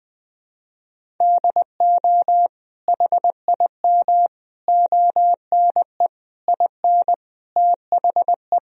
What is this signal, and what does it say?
Morse code, heard as a keyed tone, about 20 wpm, 700 Hz. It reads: DO HIM ONE IN THE
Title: Sentences from Top 1000 Words - Review Part 1 - 20wpm